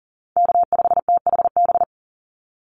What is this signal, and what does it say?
Morse code, heard as a keyed tone, about 40 wpm, 700 Hz. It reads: K5THB
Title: USA Call Signs 40wpm